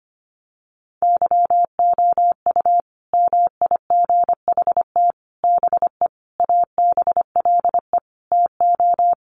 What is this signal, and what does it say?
Morse code, heard as a keyed tone, about 25 wpm, 700 Hz. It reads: YOU MIGHT BE ABLE TO
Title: Sets of 5 Words Encore2 25wpm